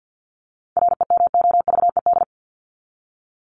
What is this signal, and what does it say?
Morse code, heard as a keyed tone, about 50 wpm, 700 Hz. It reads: REMOVED
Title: ED Suffix - Top 100 Words with Base Words 50wpm